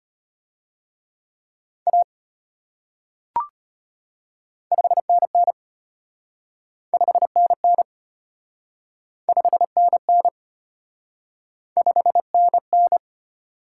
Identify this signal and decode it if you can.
Morse code, heard as a keyed tone, about 33 wpm, 700 Hz. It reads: A 5NN 5NN 5NN 5NN